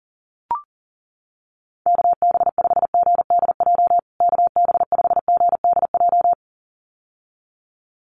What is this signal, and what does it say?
Morse code, heard as a keyed tone, about 40 wpm, 700 Hz. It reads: KB5GDJ KB5GDJ
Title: USA Call Signs Repeated 2x 40wpm